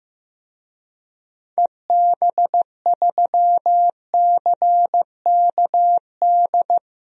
Morse code, heard as a keyed tone, about 15 wpm, 700 Hz. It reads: EB3CKD
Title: DXCC-100 Call Signs - 6 Characters - 15wpm